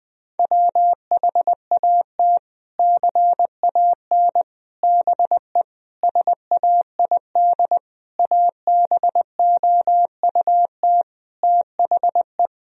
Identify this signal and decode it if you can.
Morse code, heard as a keyed tone, about 20 wpm, 700 Hz. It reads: WHAT CAN BE SAID ABOUT THE
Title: Sentences from Top 1000 Words - Review Part 4 - 20wpm